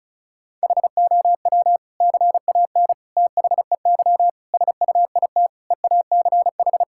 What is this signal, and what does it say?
Morse code, heard as a keyed tone, about 34 wpm, 700 Hz. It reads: HOW CAN THEY SUIT EACH
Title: Sentences from Top 1600 Words 35wpm